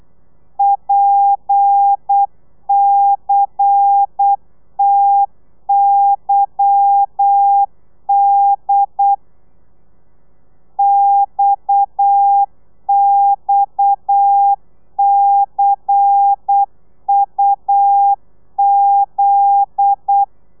Morse code, heard as a keyed tone, about 8 wpm, 800 Hz. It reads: PCTYD XXCUZ
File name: telegrafia / i2tfj / Lez10 35